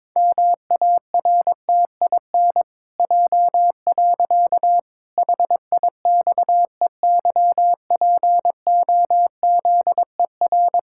Morse code, heard as a keyed tone, about 22 wpm, 700 Hz. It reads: MARTIN J. HIXEYPOZER
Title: Pangram Challenge v2 - 22wpm